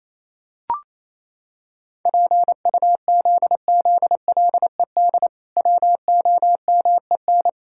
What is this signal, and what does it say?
Morse code, heard as a keyed tone, about 28 wpm, 700 Hz. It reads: PUZZLED WOMEN
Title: Pangram Challenge v2 - 28wpm